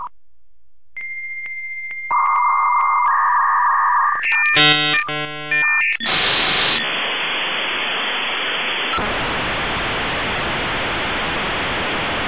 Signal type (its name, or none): dial-up modem handshake